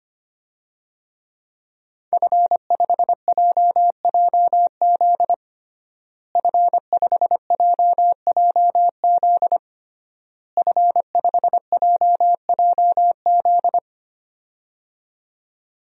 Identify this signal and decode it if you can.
Morse code, heard as a keyed tone, about 25 wpm, 700 Hz. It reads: F5JJZ F5JJZ F5JJZ
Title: DXCC-100 Call Signs - 5 Characters - Repeated 3x - 25wpm